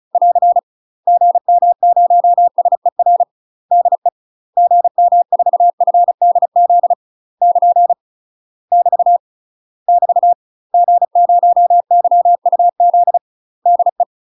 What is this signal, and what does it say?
Morse code, heard as a keyed tone, about 34 wpm, 700 Hz. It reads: <AR> GM0SER DE GM4FDZ <KN> <BT> <BT> G0YUZ DE
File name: UK MASTER QSOs TRAINING / 35WPM / UK MASTER QSOs 0004